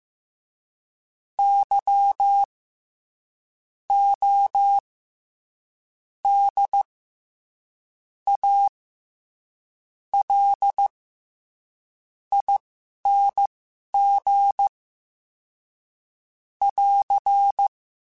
Morse code, heard as a keyed tone, about 15 wpm, spaced as for 11 wpm, 800 Hz. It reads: Y O D A L ING <AR>
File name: Special INGs-mp3s / Special INGs-mp3s / SlowToFaster ing @10